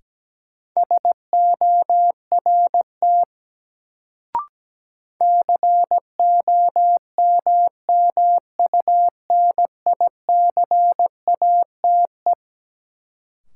Morse code, heard as a keyed tone, about 17 wpm, 700 Hz. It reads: SORT COMMUNICATE
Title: Sentences from Top 1300 Words - New Word Drill Alt - 17wpm